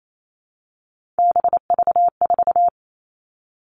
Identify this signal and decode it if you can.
Morse code, heard as a keyed tone, about 28 wpm, 700 Hz. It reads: BV4
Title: All Cut Numbers - 3-Digit 28wpm